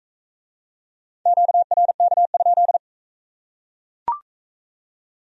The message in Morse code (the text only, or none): QRK?